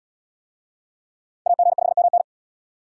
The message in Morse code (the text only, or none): AG5MN